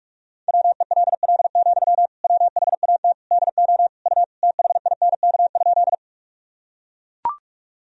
Morse code, heard as a keyed tone, about 41 wpm, 700 Hz. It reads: WELL, WHAT DO U THINK?
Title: Sentences from Top 200 Words and Abbreviations 45wpm